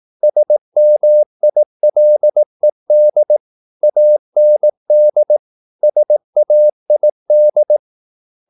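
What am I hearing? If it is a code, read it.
Morse code, heard as a keyed tone, about 18 wpm, 600 Hz. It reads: SMILED AND SAID
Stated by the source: Morse practice at 18 wpm, English text, from CW med Gnister.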